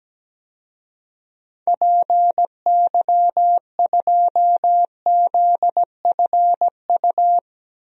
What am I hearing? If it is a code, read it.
Morse code, heard as a keyed tone, about 17 wpm, 700 Hz. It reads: PY2ZFU
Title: DXCC-100 Call Signs - 6 Characters - 17wpm